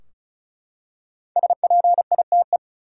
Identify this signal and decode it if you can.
Morse code, heard as a keyed tone, about 36 wpm, 700 Hz. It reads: SPITE
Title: Five Letter Words - 35wpm